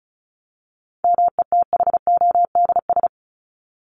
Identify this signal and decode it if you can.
Morse code, heard as a keyed tone, about 34 wpm, 700 Hz. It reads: METHODS
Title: ES:S Suffix - Top 100 Words with Base Words 35wpm